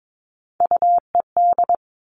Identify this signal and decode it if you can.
Morse code, heard as a keyed tone, about 22 wpm, 700 Hz. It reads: UED
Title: Three Challenging Letter-Number Combinations 22wpm